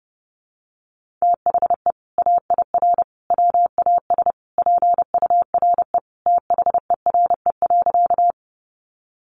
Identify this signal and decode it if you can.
Morse code, heard as a keyed tone, about 28 wpm, 700 Hz. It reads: THE AIR WAS PURE THERE.
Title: Sentences from Top 1500 Words 30wpm